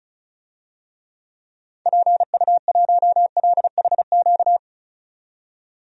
Morse code, heard as a keyed tone, about 36 wpm, 700 Hz. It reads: PU1LHQ
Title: DXCC-100 Call Signs - 6 Characters - 35wpm